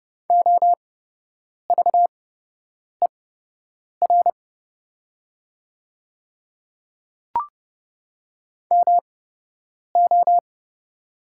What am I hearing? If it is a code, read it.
Morse code, heard as a keyed tone, about 31 wpm, spaced as for 8 wpm, 700 Hz. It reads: OVER MO
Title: 8x Character Spacing - Top 100 Words - 30wpm